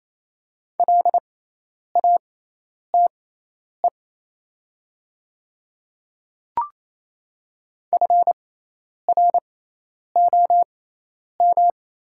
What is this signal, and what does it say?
Morse code, heard as a keyed tone, about 28 wpm, spaced as for 10 wpm, 700 Hz. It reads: LATE FROM
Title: 6x Character Spacing - Top 500 Words - 28wpm